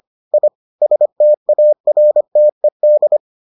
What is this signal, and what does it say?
Morse code, heard as a keyed tone, about 25 wpm, 600 Hz. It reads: I STARTED